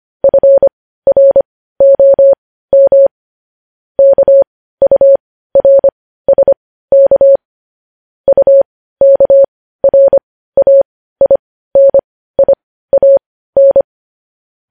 Morse code, heard as a keyed tone, about 25 wpm, spaced as for 15 wpm, 550 Hz. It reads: FROM KURSK UKRAINIAN
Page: news headlines in morse code at 15 wpm